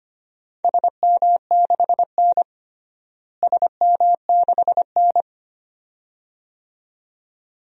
Morse code, heard as a keyed tone, about 25 wpm, 700 Hz. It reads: SM6N SM6N
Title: DXCC-100 Call Signs - 4 Characters - Repeated 2x - 25wpm